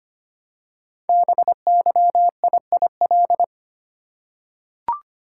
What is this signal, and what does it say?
Morse code, heard as a keyed tone, about 25 wpm, 700 Hz. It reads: BYIIL
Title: Code Groups - 5 Letters - 25wpm